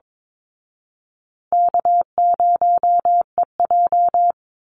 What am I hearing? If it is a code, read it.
Morse code, heard as a keyed tone, about 22 wpm, 700 Hz. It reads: K0EJ
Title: Easy Call Signs Repeated 3x - 22wpm